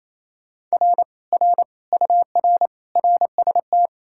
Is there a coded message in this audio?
Yes: Morse code from 0.7 to 3.9 s, heard as a keyed tone, about 37 wpm, 700 Hz. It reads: R R UR RST